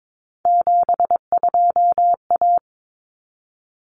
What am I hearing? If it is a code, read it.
Morse code, heard as a keyed tone, about 22 wpm, 700 Hz. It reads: 72A
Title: Common Cut Numbers - 3-Digit 22wpm